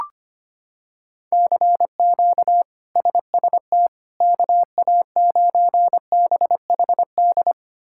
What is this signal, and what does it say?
Morse code, heard as a keyed tone, about 25 wpm, 700 Hz. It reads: CQ SST KA9BHD
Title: K1USN SST QSO Practice - 25wpm